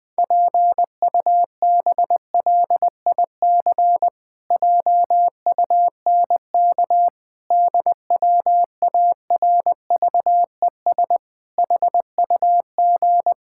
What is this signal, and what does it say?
Morse code, heard as a keyed tone, about 20 wpm, 700 Hz. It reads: PUBLIC JUNK DWARVES HUG
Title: Pangram Challenge v2 - 20wpm